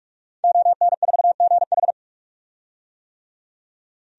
Morse code, heard as a keyed tone, about 45 wpm, 700 Hz. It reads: ON4GH
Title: DXCC-100 Call Signs - 5 Characters - 45wpm